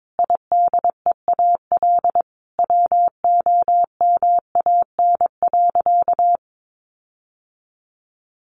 Morse code, heard as a keyed tone, about 22 wpm, 700 Hz. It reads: IDEAL WOMAN.
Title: Sentences from Top 1400 Words 22wpm